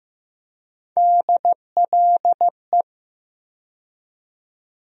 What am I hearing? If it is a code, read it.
Morse code, heard as a keyed tone, about 15 wpm, 700 Hz. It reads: DLE
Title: USA Call Sign Postfixes - 3 Letters 15wpm